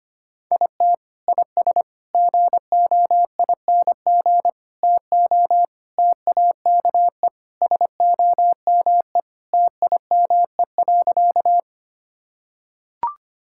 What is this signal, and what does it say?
Morse code, heard as a keyed tone, about 25 wpm, 700 Hz. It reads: IT IS GOING TO TAKE SOME TIME.
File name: practice / 25wpm / Sentences with 3-Word Phrases - 25wpm